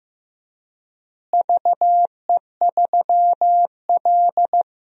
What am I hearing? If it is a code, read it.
Morse code, heard as a keyed tone, about 16 wpm, 700 Hz. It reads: VE3L
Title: DXCC-100 Call Signs - 4 Characters - Repeated 3x - 15wpm